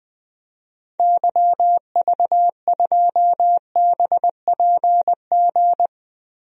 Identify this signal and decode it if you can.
Morse code, heard as a keyed tone, about 20 wpm, 700 Hz. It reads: YV2BPG
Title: DXCC-100 Call Signs - 6 Characters - Repeated 2x - 20wpm